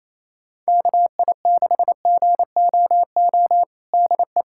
Morse code, heard as a keyed tone, about 28 wpm, 700 Hz. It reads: KI6GOO DE
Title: USA Callsign DE Callsign - 6 Characters - 28wpm